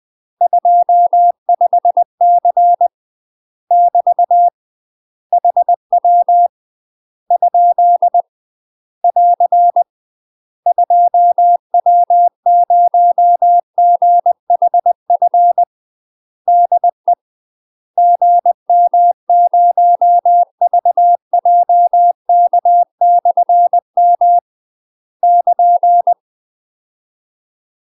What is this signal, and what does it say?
Morse code, heard as a keyed tone, about 20 wpm, 700 Hz. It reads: 25C <BT> HW ? <AR> 2W0GHF DE GM0VJK/M <KN>
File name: UK MASTER QSOs TRAINING/20WPM